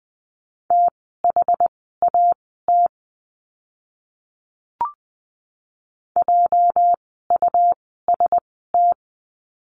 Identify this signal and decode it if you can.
Morse code, heard as a keyed tone, about 20 wpm, spaced as for 14 wpm, 700 Hz. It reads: THAT JUST
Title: 237 Lesson 38f - Letter q - Words 20wpm